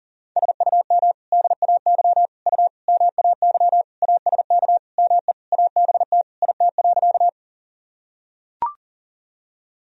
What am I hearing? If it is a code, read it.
Morse code, heard as a keyed tone, about 37 wpm, 700 Hz. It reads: SUM DAY U MAY ASK ME ABT IT.
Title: Sentences from Top 200 Words and Abbreviations 40wpm